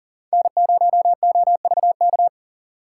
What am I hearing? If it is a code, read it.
Morse code, heard as a keyed tone, about 40 wpm, 700 Hz. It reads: N0OVK
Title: USA Call Signs - 5 Characters - 2x Character Spacing - 40wpm